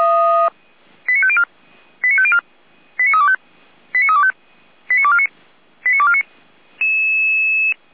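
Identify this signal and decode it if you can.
Five-tone selective call (ZVEI), heard as quick runs of five tones, tones of about 70 ms. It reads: 89504 89504 89325 89325 89249 89249
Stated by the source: ZVEI five-tone selective call received by SDR, feuerwehr-orlamuende.org fire brigade Orlamünde.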